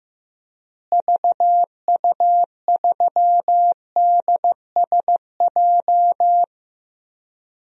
Morse code, heard as a keyed tone, about 15 wpm, 700 Hz. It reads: VU3DSJ
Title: DXCC-100 Call Signs - 6 Characters - Repeated 3x - 15wpm